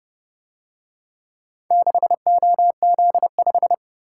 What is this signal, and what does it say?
Morse code, heard as a keyed tone, about 30 wpm, 700 Hz. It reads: 6OZ5